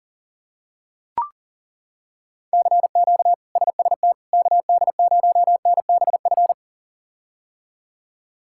Morse code, heard as a keyed tone, about 40 wpm, 700 Hz. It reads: CQ SST KD0NBF
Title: K1USN SST QSO Practice - 40wpm